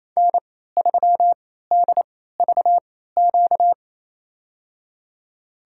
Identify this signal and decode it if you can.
Morse code, heard as a keyed tone, about 27 wpm, spaced as for 16 wpm, 700 Hz. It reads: N3DVQ
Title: USA Call Signs - 3x Character Spacing - 28wpm